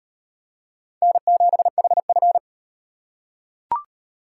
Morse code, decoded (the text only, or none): N7HF